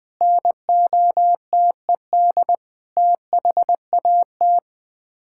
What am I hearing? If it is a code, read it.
Morse code, heard as a keyed tone, about 20 wpm, 700 Hz. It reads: NOTED THAT
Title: Sets of 6 Words - 20wpm